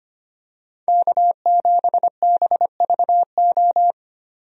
Morse code, heard as a keyed tone, about 25 wpm, 700 Hz. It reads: K7BVO
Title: USA Call Signs - 5 Characters - Repeated 2x - 25wpm